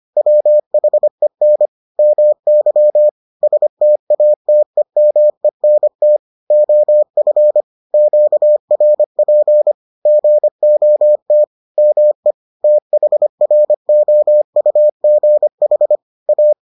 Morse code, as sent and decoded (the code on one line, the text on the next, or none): .-- .... -. / -- -.-- / ... - .- - . -- . -. - / --- ..-. / --.- .-. .--. / --. --- - / -- . / - .... .-. --- ..- --. .... / .-
WHEN MY STATEMENT OF QRP GOT ME THROUGH A